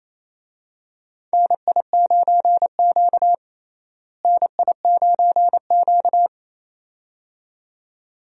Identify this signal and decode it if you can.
Morse code, heard as a keyed tone, about 28 wpm, 700 Hz. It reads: NI9Q NI9Q
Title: USA Call Signs - 4 Characters - Repeated 2x - 28wpm